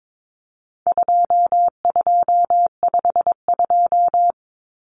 Morse code, heard as a keyed tone, about 22 wpm, 700 Hz. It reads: 2252
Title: Common Cut Numbers - 4-Digit 22wpm